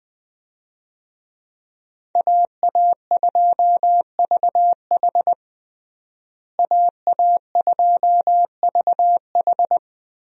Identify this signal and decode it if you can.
Morse code, heard as a keyed tone, about 20 wpm, 700 Hz. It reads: AA2VH AA2VH
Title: USA Call Signs - 5 Characters - Repeated 3x - 20wpm